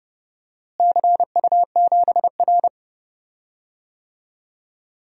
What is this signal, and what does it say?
Morse code, heard as a keyed tone, about 30 wpm, 700 Hz. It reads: CU7R
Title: DXCC-100 Call Signs - 4 Characters - Repeated 2x - 30wpm